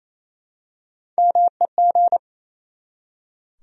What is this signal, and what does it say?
Morse code, heard as a keyed tone, about 28 wpm, 700 Hz. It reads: MEG
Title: USA Common Names - Max 3 Letters - 28wpm